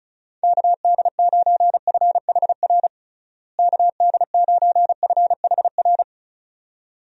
Morse code, heard as a keyed tone, about 36 wpm, 700 Hz. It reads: KD9FHR KD9FHR
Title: USA Call Signs - 6 Characters - Repeated 2x - 35wpm